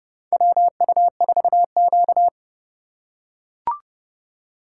Morse code, heard as a keyed tone, about 30 wpm, 700 Hz. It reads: WU4Q
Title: USA Call Signs - 4 Characters - 2x Character Spacing - 30wpm